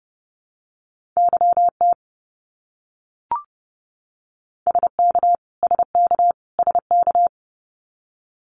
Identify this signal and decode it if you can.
Morse code, heard as a keyed tone, about 30 wpm, 700 Hz. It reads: YT SK SK SK
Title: Canadian Provinces and Territories - Repeated 3x - 30wpm